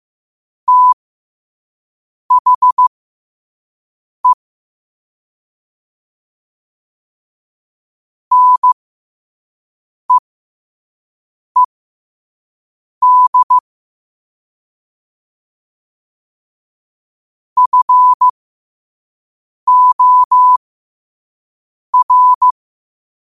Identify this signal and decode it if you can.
Morse code, heard as a keyed tone, about 15 wpm, spaced as for 5 wpm, 1000 Hz. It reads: THE NEED FOR
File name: qotdfiles / 5wpm